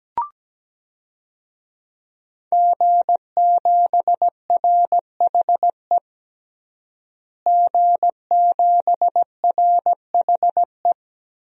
Morse code, heard as a keyed tone, about 17 wpm, 700 Hz. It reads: G7RHE G7RHE
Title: DXCC-100 Call Signs - 5 Characters - Repeated 3x - 17wpm